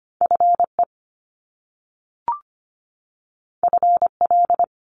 Morse code, heard as a keyed tone, about 25 wpm, 700 Hz. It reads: FE FL
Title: Top Two Letter Combinations in Words 25wpm